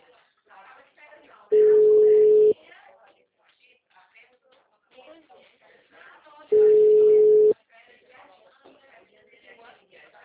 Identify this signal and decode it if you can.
Ringback tone (Europe, 425 Hz), with speech.